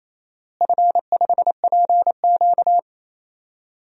Morse code, heard as a keyed tone, about 28 wpm, 700 Hz. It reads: F5PQ
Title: DXCC-100 Call Signs - 4 Characters - Repeated 3x - 28wpm